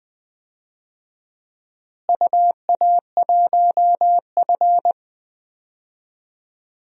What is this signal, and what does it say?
Morse code, heard as a keyed tone, about 20 wpm, 700 Hz. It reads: UA1F
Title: DXCC-100 Call Signs - 4 Characters - Repeated 2x - 20wpm